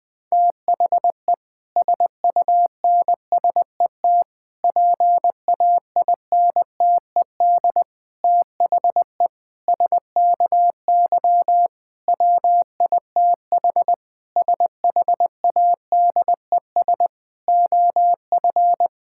Morse code, heard as a keyed tone, about 20 wpm, 700 Hz. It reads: THE SUNSET PAINTED THE SKY WITH SHADES OF
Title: Difficult Sentences with Hints - 20wpm